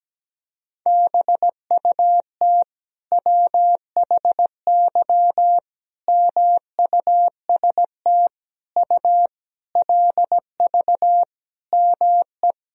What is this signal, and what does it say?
Morse code, heard as a keyed tone, about 17 wpm, 700 Hz. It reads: BUT WHY MUST U LV ME
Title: Sentences from Top 400 Words and Abbreviations 17wpm